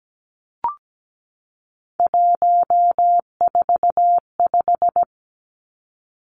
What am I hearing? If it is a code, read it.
Morse code, heard as a keyed tone, about 17 wpm, 700 Hz. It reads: 145
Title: Numbers - 3-Digit 17wpm